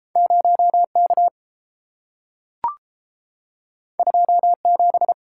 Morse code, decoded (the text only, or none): 0K 27